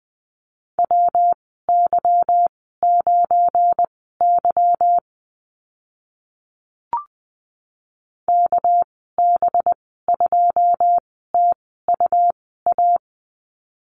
Morse code, heard as a keyed tone, about 20 wpm, spaced as for 14 wpm, 700 Hz. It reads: WY9Y KB2TUA
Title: 207 Lesson 33g - Letter j - Call Signs 20wpm